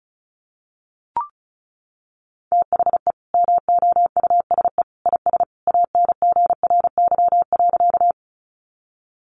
Morse code, heard as a keyed tone, about 34 wpm, 700 Hz. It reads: THE MOUSE IS ANGRY.
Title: Easy Phrases 35wpm